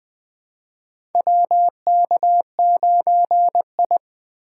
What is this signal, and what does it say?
Morse code, heard as a keyed tone, about 20 wpm, 700 Hz. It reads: WK9I
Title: USA Call Signs - 4 Characters - 3x Character Spacing - 20wpm